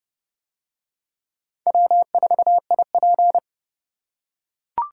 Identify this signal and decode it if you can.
Morse code, heard as a keyed tone, about 32 wpm, 700 Hz. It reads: W4IP